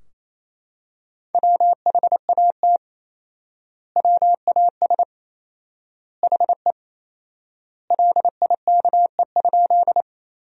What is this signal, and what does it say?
Morse code, heard as a keyed tone, about 28 wpm, 700 Hz. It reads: WHAT WAS HE LIKE?
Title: 4x Word-Spacing - Sentences from Top 100 Words - 28wpm